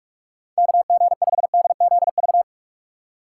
Morse code, heard as a keyed tone, about 44 wpm, 700 Hz. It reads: KG5DZV
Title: USA Call Signs - 6 Characters - 3x Character Spacing - 45wpm